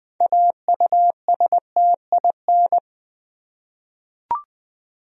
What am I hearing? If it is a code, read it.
Morse code, heard as a keyed tone, about 20 wpm, 700 Hz. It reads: AUSTIN
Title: USA Common Names - 20wpm